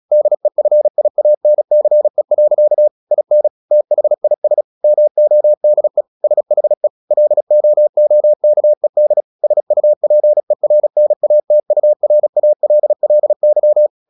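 Morse code, heard as a keyed tone, about 36 wpm, 600 Hz. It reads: DEFIANCE. IN THIS MODE SHE LOOKED SUPERNATURALLY
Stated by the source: Morse practice at 36 wpm, text of Ken Follett's novel Never, from CW med Gnister.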